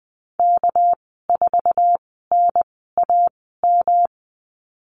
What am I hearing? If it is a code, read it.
Morse code, heard as a keyed tone, about 20 wpm, spaced as for 14 wpm, 700 Hz. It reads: K4NAM
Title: 219 Lesson 35g - Number 0 - Call Signs 20wpm